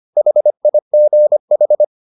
Morse code, heard as a keyed tone, about 25 wpm, 600 Hz. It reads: HIGH